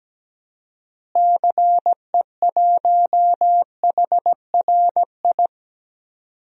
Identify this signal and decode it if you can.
Morse code, heard as a keyed tone, about 17 wpm, 700 Hz. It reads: CE1HRI
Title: DXCC-100 Call Signs - 6 Characters - Repeated 2x - 17wpm